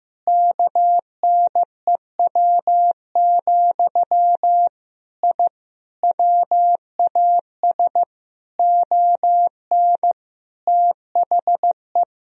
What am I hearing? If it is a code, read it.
Morse code, heard as a keyed tone, about 15 wpm, 700 Hz. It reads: KNEW, I WAS ON THE
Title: Sentences from Top 1000 Words - Review Part 3 - 15wpm